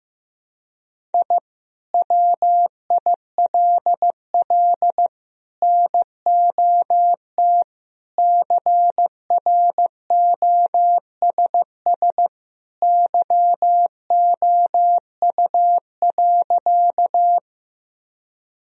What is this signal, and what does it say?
Morse code, heard as a keyed tone, about 16 wpm, 700 Hz. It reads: I WILL NOT CROSS YOU.
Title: Sentences from Top 1000 Words - Review Part 1 - 15wpm